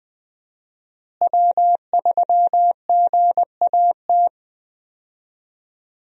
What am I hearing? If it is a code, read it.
Morse code, heard as a keyed tone, about 20 wpm, 700 Hz. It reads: W3GAT
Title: USA Call Signs - 5 Characters - Repeated 3x - 20wpm